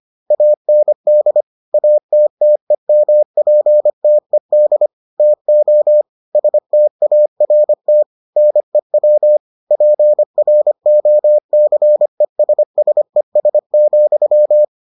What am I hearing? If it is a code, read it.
Morse code, heard as a keyed tone, about 25 wpm, 600 Hz. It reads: AND ATTEMPTED TO START NEW PROCESSES,